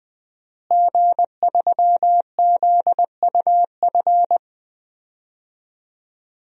Morse code, heard as a keyed tone, about 20 wpm, 700 Hz. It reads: G3ZUF
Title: DXCC-100 Call Signs - 5 Characters - Repeated 2x - 20wpm